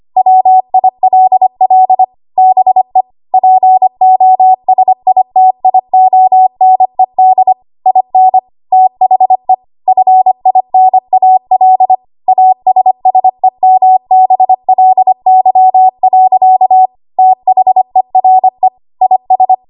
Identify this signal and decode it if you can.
Morse code, heard as a keyed tone, about 25 wpm, 750 Hz. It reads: WILL BE POSITIONED IN THE FINAL ASSEMBLY. THERE IS
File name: Morse / 25WPM